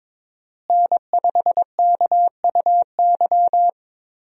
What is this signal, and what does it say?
Morse code, heard as a keyed tone, about 22 wpm, 700 Hz. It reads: N5KUY
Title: USA Call Signs - 5 Characters - Repeated 3x - 22wpm